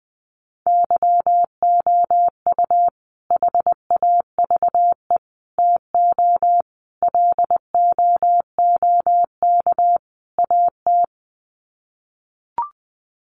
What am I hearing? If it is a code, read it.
Morse code, heard as a keyed tone, about 20 wpm, 700 Hz. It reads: YOU HAVE TO LOOK AT
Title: Sets of 5 Words Encore 20wpm